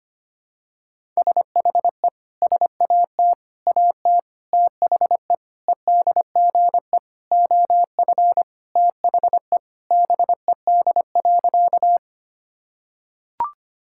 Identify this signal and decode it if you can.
Morse code, heard as a keyed tone, about 25 wpm, 700 Hz. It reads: SHE SAT AT THE EDGE OF THE BED.
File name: practice / 25wpm / Sentences with 4-Word Phrases - 25wpm